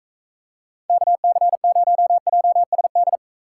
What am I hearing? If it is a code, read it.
Morse code, heard as a keyed tone, about 44 wpm, 700 Hz. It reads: KC0JSD